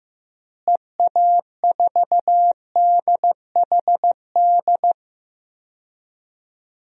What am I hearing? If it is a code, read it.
Morse code, heard as a keyed tone, about 15 wpm, 700 Hz. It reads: EA4DHD
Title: DXCC-100 Call Signs - 6 Characters - Repeated 2x - 15wpm